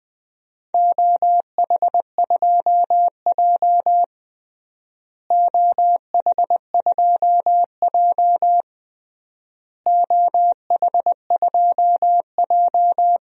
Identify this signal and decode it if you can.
Morse code, heard as a keyed tone, about 20 wpm, 700 Hz. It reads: OH2J OH2J OH2J
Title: DXCC-100 Call Signs - 4 Characters - Repeated 3x - 20wpm